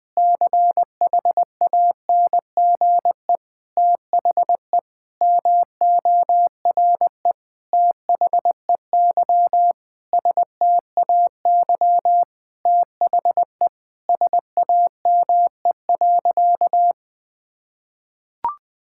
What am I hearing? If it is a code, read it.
Morse code, heard as a keyed tone, about 20 wpm, 700 Hz. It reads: CHANGE THE MORE THEY STAY THE SAME.
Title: Sentences with 9-Word Phrases - Spoken Beginning - 20wpm